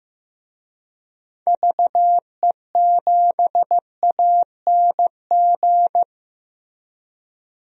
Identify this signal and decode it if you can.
Morse code, heard as a keyed tone, about 15 wpm, 700 Hz. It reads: VE7ANG
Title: DXCC-100 Call Signs - 6 Characters - Repeated 3x - 15wpm